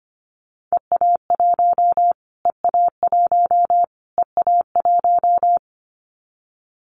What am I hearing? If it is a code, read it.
Morse code, heard as a keyed tone, about 25 wpm, 700 Hz. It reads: EA1 EA1 EA1
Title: Callsign Prefixes for Top 25 DXCC - Repeated 3x - 25wpm